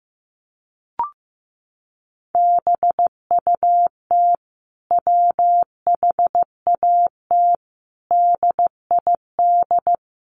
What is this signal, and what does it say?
Morse code, heard as a keyed tone, about 15 wpm, 700 Hz. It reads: BUT WHAT DID
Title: Sentences from Top 100 Words 15wpm